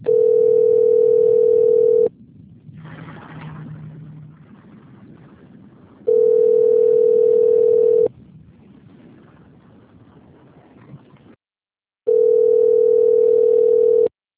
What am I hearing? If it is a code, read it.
Ringback tone (North America), 440 + 480 Hz.